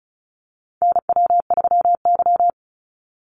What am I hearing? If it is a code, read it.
Morse code, heard as a keyed tone, about 34 wpm, 700 Hz. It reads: NW3Y
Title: Call Signs Repeated 2x - 35wpm